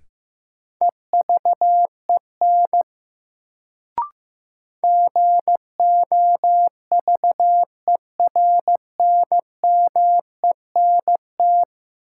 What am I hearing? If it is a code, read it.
Morse code, heard as a keyed tone, about 15 wpm, 700 Hz. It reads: EVEN GOVERNMENT
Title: Sentences from Top 600 Words - New Word Drill - 15wpm